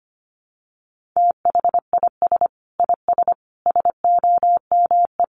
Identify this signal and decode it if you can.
Morse code, heard as a keyed tone, about 25 wpm, 700 Hz. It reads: THIS IS SOME
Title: Sentences from Top 1400 Words 25wpm